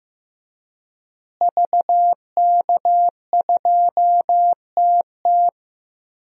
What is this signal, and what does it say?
Morse code, heard as a keyed tone, about 15 wpm, 700 Hz. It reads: VK2TT
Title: DXCC-100 Call Signs - 5 Characters - Repeated 3x - 15wpm